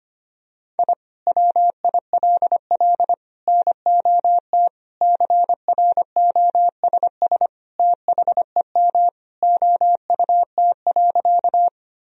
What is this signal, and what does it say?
Morse code, heard as a keyed tone, about 25 wpm, 700 Hz. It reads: I WILL NOT CROSS THEM OUT.
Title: Sentences from Top 1000 Words - Review Part 1 - 25wpm